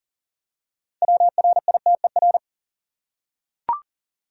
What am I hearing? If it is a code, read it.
Morse code, heard as a keyed tone, about 40 wpm, 700 Hz. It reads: WRITER
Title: Six Letter Words - 40wpm